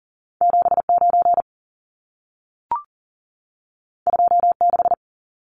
Morse code, heard as a keyed tone, about 40 wpm, 700 Hz. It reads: 79 26
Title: Numbers - 2-Digit 40wpm